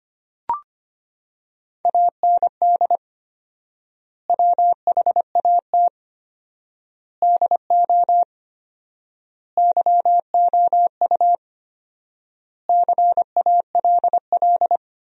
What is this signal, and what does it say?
Morse code, heard as a keyed tone, about 25 wpm, 700 Hz. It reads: AND WHAT DO YOU CALL
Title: 4x Word-Spacing - Sentences from Top 100 Words - 25wpm